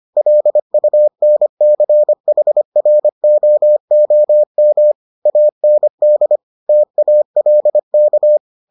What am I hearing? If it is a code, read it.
Morse code, heard as a keyed tone, about 25 wpm, 600 Hz. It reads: LUNCHROOM AND TALK